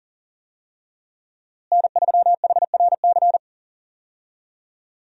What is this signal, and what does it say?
Morse code, heard as a keyed tone, about 40 wpm, 700 Hz. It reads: N3HRC
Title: USA Call Signs - 5 Characters - Repeated 3x - 40wpm